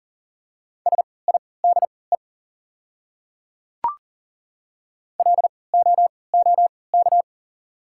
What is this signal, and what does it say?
Morse code, heard as a keyed tone, about 40 wpm, spaced as for 23 wpm, 690 Hz. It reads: SIDE LOOK
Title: 3x Character Spacing - Top 100 Words - 40wpm